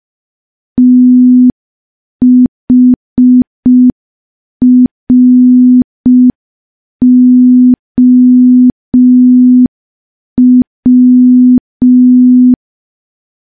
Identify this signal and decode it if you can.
Morse code, heard as a keyed tone, about 5 wpm, 250 Hz. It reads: THROW
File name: morse / morse-training-05WPM